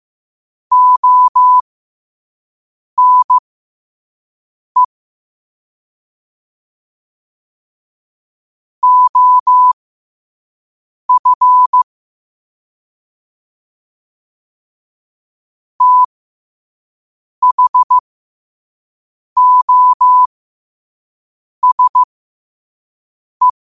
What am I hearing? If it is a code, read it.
Morse code, heard as a keyed tone, about 15 wpm, spaced as for 5 wpm, 1000 Hz. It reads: ONE OF THOSE